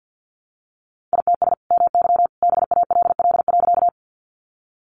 Morse code, heard as a keyed tone, about 49 wpm, 700 Hz. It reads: ITS MY BALL.
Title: Sentences from Top 900 Words 50wpm